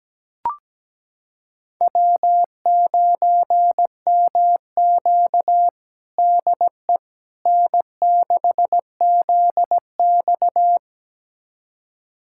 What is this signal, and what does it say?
Morse code, heard as a keyed tone, about 17 wpm, 700 Hz. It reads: W9MQ DE N6ZX
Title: USA Callsign DE Callsign - 4 Characters - 17wpm